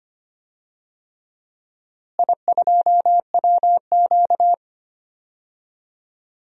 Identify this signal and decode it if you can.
Morse code, heard as a keyed tone, about 25 wpm, 700 Hz. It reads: I2WQ
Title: DXCC-100 Call Signs - 4 Characters - Repeated 3x - 25wpm